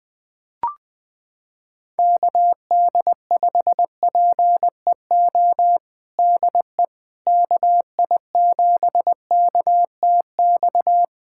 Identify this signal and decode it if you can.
Morse code, heard as a keyed tone, about 20 wpm, 700 Hz. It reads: KD5PEO DE KI7KTX
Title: USA Callsign DE Callsign - 6 Characters - 20wpm